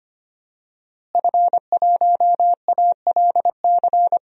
Morse code, heard as a keyed tone, about 25 wpm, 700 Hz. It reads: F1ALC
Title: DXCC-100 Call Signs - 5 Characters - Repeated 3x - 25wpm